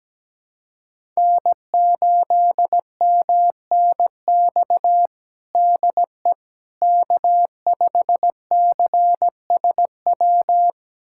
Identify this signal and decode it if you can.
Morse code, heard as a keyed tone, about 17 wpm, 700 Hz. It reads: N8MNX DE K5CSW
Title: USA Callsign DE Callsign 17wpm